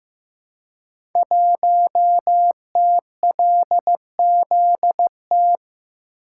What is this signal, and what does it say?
Morse code, heard as a keyed tone, about 15 wpm, 700 Hz. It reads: 1TLZT
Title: Code Groups - 5 Letter:Numbers - 15wpm